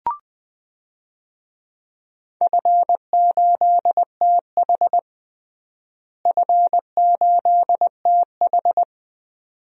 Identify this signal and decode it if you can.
Morse code, heard as a keyed tone, about 20 wpm, 700 Hz. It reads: F8TH F8TH
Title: DXCC-100 Call Signs - 4 Characters - Repeated 2x - 20wpm